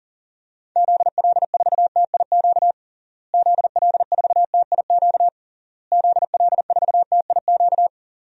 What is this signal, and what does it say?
Morse code, heard as a keyed tone, about 37 wpm, 700 Hz. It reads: ZL4TIQ ZL4TIQ ZL4TIQ
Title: DXCC-100 Call Signs - 6 Characters - Repeated 3x - 40wpm